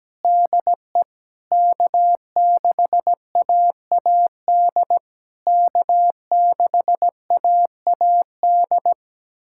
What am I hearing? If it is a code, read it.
Morse code, heard as a keyed tone, about 17 wpm, 700 Hz. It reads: DE K6AAD K6AAD
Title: SOTA QSO Practice 17wpm